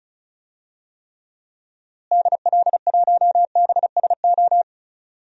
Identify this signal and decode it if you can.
Morse code, heard as a keyed tone, about 36 wpm, 700 Hz. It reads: DL1BSO
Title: DXCC-100 Call Signs - 6 Characters - Repeated 2x - 35wpm